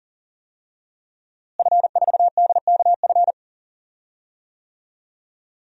Morse code, heard as a keyed tone, about 40 wpm, 700 Hz. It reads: F4DKF
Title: DXCC-100 Call Signs - 5 Characters - Repeated 2x - 40wpm